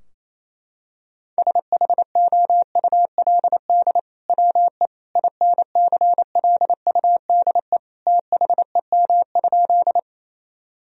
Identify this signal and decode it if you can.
Morse code, heard as a keyed tone, about 28 wpm, 700 Hz. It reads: SHOULD WE INCLUDE THEM?
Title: Sentences from Top 1000 Words - Review Part 3 - 28wpm